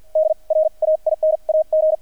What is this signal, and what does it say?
Morse code, heard as a keyed tone, about 50 wpm, spaced as for 39 wpm, 630 Hz. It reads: DRAIN AG
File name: archived / CRSnet